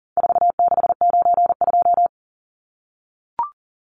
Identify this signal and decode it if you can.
Morse code, heard as a keyed tone, about 40 wpm, 700 Hz. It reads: 4692
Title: Numbers - 4-Digit 40wpm